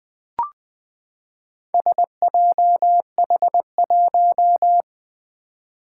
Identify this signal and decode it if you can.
Morse code, heard as a keyed tone, about 20 wpm, 700 Hz. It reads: SJH1
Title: Four Challenging Letter-Number Combinations 20wpm